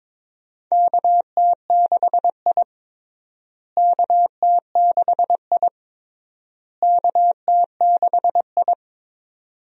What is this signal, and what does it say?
Morse code, heard as a keyed tone, about 22 wpm, 700 Hz. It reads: KT6I KT6I KT6I
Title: USA Call Signs - 4 Characters - Repeated 3x - 22wpm